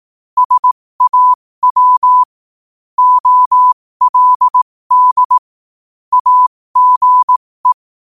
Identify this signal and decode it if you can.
Morse code, heard as a keyed tone, about 18 wpm, spaced as for 15 wpm, 1000 Hz. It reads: SAW OLD AGE
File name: qotdfiles / 15wpm